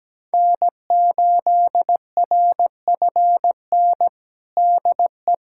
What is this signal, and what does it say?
Morse code, heard as a keyed tone, about 17 wpm, 700 Hz. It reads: N8RFN DE
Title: USA Callsign DE Callsign - 5 Characters - 17wpm